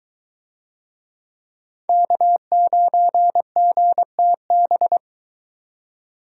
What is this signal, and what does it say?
Morse code, heard as a keyed tone, about 23 wpm, 700 Hz. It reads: K9GTB